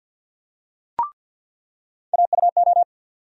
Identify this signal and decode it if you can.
Morse code, heard as a keyed tone, about 47 wpm, 700 Hz. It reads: AUO
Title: USA Call Sign Postfixes - 3 Letters 50wpm